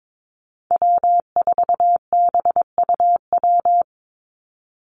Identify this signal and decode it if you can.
Morse code, heard as a keyed tone, about 22 wpm, 700 Hz. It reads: W4BUW
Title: Call Signs Repeated 2x - 22wpm